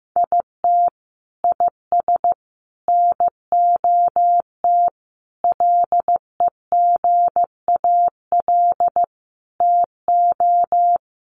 Morse code, heard as a keyed tone, about 15 wpm, 700 Hz. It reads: IT IS NOT LEGAL TO
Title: Sentences from Top 1200 Words 15wpm